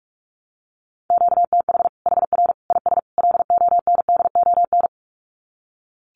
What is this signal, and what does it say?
Morse code, heard as a keyed tone, about 44 wpm, 700 Hz. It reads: QTH HR IS LONDON
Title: Common QSO Phrases 45wpm